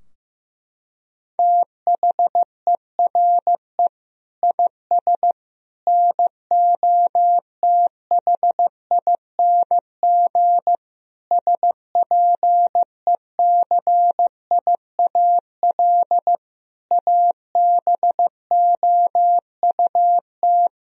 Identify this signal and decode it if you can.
Morse code, heard as a keyed tone, about 16 wpm, 700 Hz. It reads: THERE IS NOTHING SPECIAL ABOUT
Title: Sentences from Top 1000 Words - Review Part 3 - 15wpm